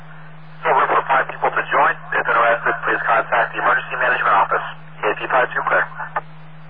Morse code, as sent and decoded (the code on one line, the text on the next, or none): none